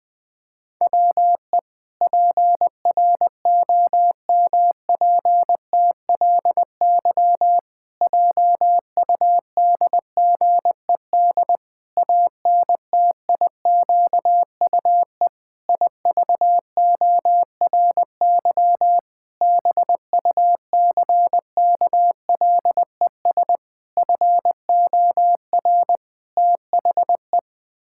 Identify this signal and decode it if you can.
Morse code, heard as a keyed tone, about 20 wpm, 700 Hz. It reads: WE PROMPTLY JUDGED ANTIQUE IVORY BUCKLES FOR THE
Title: Pangram Challenge v2 - 20wpm